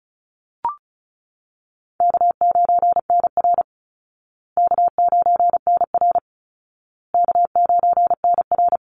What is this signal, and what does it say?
Morse code, heard as a keyed tone, about 34 wpm, 700 Hz. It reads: K9NR K9NR K9NR
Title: Easy Call Signs Repeated 3x - 35wpm